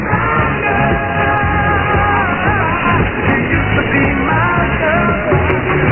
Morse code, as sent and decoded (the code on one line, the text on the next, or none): none